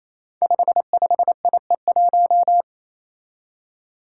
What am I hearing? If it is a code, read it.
Morse code, heard as a keyed tone, about 27 wpm, 700 Hz. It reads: 55IE1
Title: Five Challenging Letter-Number Combinations 28wpm